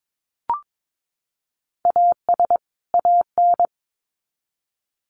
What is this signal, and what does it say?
Morse code, heard as a keyed tone, about 22 wpm, 700 Hz. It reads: AS AN
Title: Sets of 2 Words 22wpm